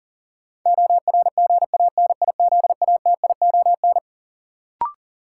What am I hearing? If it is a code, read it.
Morse code, heard as a keyed tone, about 40 wpm, 700 Hz. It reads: ORGANIZATION
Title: Sentences from Top 1100 Words - New Word Drill Alt - 40wpm